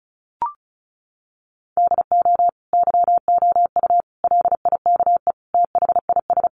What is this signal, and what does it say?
Morse code, heard as a keyed tone, about 34 wpm, 700 Hz. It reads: DO YOU LIKE THIS
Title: Sentences from Top 1200 Words 35wpm